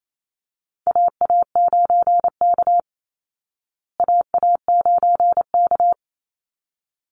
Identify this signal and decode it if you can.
Morse code, heard as a keyed tone, about 28 wpm, 700 Hz. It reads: AA9K AA9K
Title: Call Signs Repeated 3x - 28wpm